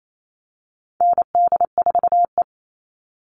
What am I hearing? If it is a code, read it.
Morse code, heard as a keyed tone, about 28 wpm, 700 Hz. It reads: ND4E
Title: All Cut Numbers - 4-Digit 28wpm